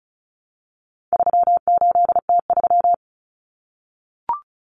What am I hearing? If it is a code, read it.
Morse code, heard as a keyed tone, about 34 wpm, 700 Hz. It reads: 38T3
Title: Common Cut Numbers - 4-Digit 35wpm